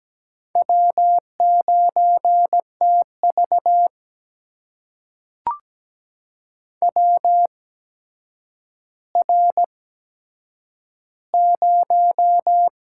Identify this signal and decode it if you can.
Morse code, heard as a keyed tone, about 17 wpm, spaced as for 8 wpm, 700 Hz. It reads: W9TV WR0
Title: USA Call Signs - 4 Characters - 8x Character Spacing - 17wpm